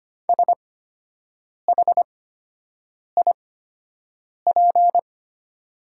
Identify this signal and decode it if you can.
Morse code, heard as a keyed tone, about 25 wpm, spaced as for 7 wpm, 700 Hz. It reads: SHIP
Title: 8x Character Spacing - Top 500 Words - 25wpm